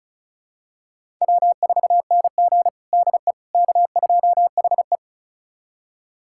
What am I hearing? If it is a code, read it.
Morse code, heard as a keyed tone, about 36 wpm, 700 Hz. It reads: W4NG DE K2HE
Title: USA Callsign DE Callsign - 4 Characters - 35wpm